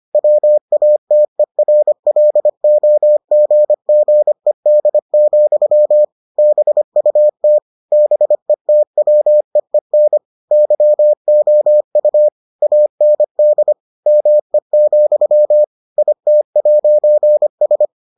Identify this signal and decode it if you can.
Morse code, heard as a keyed tone, about 25 wpm, 600 Hz. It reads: WATERLOGGED, BUT BETWEEN YOU AND ME, IT'S